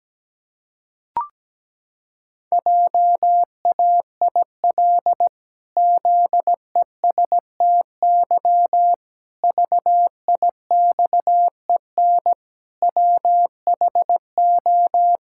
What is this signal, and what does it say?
Morse code, heard as a keyed tone, about 17 wpm, 700 Hz. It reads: JAIL ZESTY VIXEN WHO
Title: Pangram Challenge v2 - 17wpm